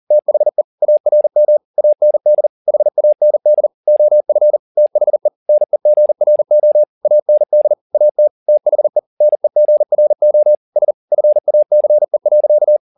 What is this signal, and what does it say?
Morse code, heard as a keyed tone, about 40 wpm, 600 Hz. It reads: THE ARM AND HAND OF THE NEGRO AND AT THE NEGRO S FACE.
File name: the old man and the sea 2 40wpm